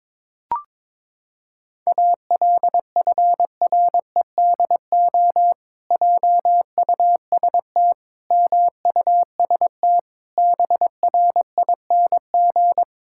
Morse code, heard as a keyed tone, about 22 wpm, 700 Hz. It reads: ALFREDO JUST MUST BRING
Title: Pangram Challenge v2 - 22wpm